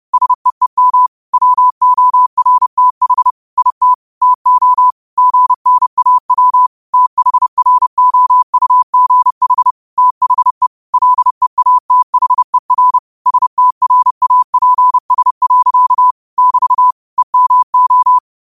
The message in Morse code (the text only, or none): SEEM WORTH IT TO GNAW THROUGH THE LEATHER STRAPS. <BT> EMO